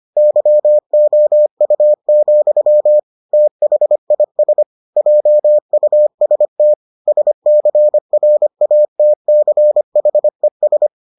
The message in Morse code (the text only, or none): YOU, THIS JUST SCRATCHES